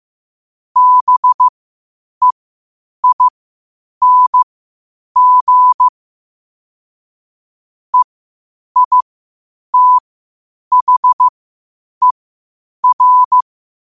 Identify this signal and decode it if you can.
Morse code, heard as a keyed tone, about 15 wpm, spaced as for 8 wpm, 1000 Hz. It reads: BEING EITHER
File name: qotdfiles / 8wpm